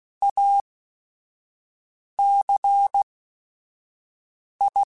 Morse code, heard as a keyed tone, about 16 wpm, spaced as for 5 wpm, 800 Hz. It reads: ACI
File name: Technical Subjects / Technical Subjects-mp3s / XC @5